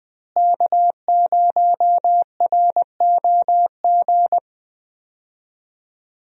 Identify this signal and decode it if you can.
Morse code, heard as a keyed tone, about 19 wpm, 700 Hz. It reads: K0ROG
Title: USA Call Signs - 5 Characters - Repeated 3x - 20wpm